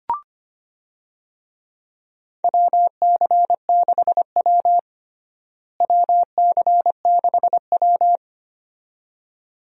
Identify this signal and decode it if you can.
Morse code, heard as a keyed tone, about 25 wpm, 700 Hz. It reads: WC6W WC6W
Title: USA Call Signs - 4 Characters - Repeated 2x - 25wpm